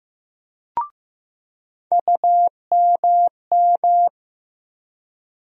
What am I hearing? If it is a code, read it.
Morse code, heard as a keyed tone, about 15 wpm, 700 Hz. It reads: UMM
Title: USA Call Sign Postfixes - 3 Letters 15wpm